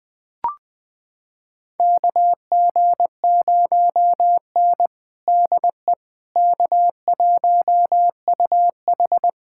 Morse code, heard as a keyed tone, about 20 wpm, 700 Hz. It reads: KG0N DE K1UH
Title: USA Callsign DE Callsign - 4 Characters - 20wpm